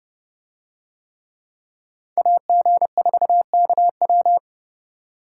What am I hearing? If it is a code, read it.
Morse code, heard as a keyed tone, about 27 wpm, 700 Hz. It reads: AG4KW